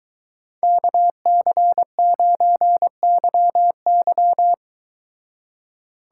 Morse code, heard as a keyed tone, about 23 wpm, 700 Hz. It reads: KC9YY